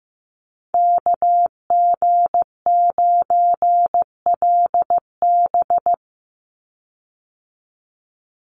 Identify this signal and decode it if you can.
Morse code, heard as a keyed tone, about 15 wpm, 700 Hz. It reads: KG9LB
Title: Call Signs Repeated 2x - 15wpm